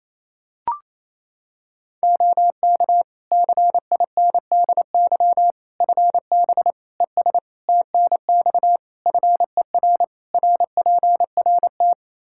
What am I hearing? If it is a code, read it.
Morse code, heard as a keyed tone, about 28 wpm, 700 Hz. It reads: OK CINDY FB ES TNX FER RPRT
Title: Practice QSO - Exchange 2 - Info - 28wpm